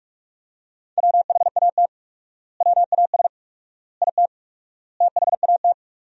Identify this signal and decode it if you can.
Morse code, heard as a keyed tone, about 43 wpm, 700 Hz. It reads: WHAT WAS IT THAT
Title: 4x Word-Spacing - Sentences from Top 300 Words - 45wpm